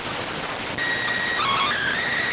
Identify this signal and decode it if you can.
SSTV (slow-scan television), PD 290.
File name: findings / CT SAGITAL 27.gen-18 rx